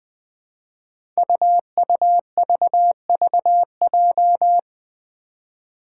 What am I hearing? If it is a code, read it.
Morse code, heard as a keyed tone, about 20 wpm, 700 Hz. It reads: UUVVJ
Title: Five Challenging Letter-Number Combinations 20wpm